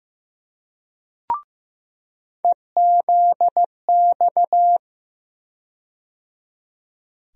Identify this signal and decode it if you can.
Morse code, heard as a keyed tone, about 15 wpm, 700 Hz. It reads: EZX
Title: USA Call Sign Postfixes - 3 Letters 15wpm